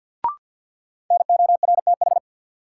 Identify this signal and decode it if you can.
Morse code, heard as a keyed tone, about 50 wpm, 700 Hz. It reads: NORTH
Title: Sets of 5 Words Encore - New Word Drill - 50wpm